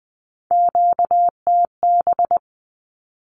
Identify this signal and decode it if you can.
Morse code, heard as a keyed tone, about 20 wpm, 700 Hz. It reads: QTB
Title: Q-Codes with Spoken Meaning - 20wpm